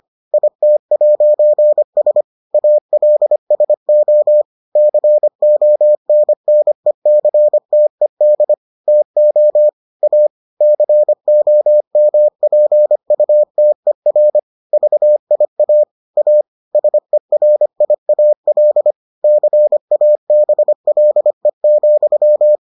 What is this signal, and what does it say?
Morse code, heard as a keyed tone, about 25 wpm, 600 Hz. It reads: IT'S ALSO CONNECTED TO A COMPUTER VIA A SERIAL CABLE,